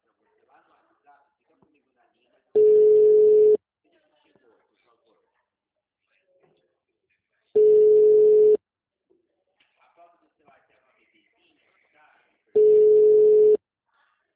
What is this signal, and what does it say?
Ringback tone (Europe, 425 Hz), with speech.
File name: out- -2013- - - .18799